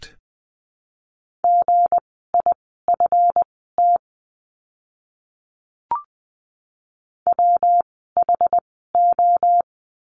Morse code, heard as a keyed tone, about 20 wpm, spaced as for 15 wpm, 700 Hz. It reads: GIFT WHO
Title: 160 Lesson 26f - Letter g - Words 20wpm